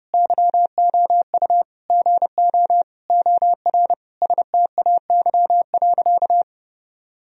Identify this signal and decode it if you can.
Morse code, heard as a keyed tone, about 30 wpm, 700 Hz. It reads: YOU GO OR STAY.
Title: Sentences from Top 1000 Words - Review Part 3 - 30wpm